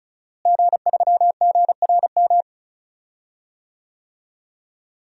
Morse code, heard as a keyed tone, about 36 wpm, 700 Hz. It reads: G3GRM
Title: DXCC-100 Call Signs - 5 Characters - Repeated 3x - 35wpm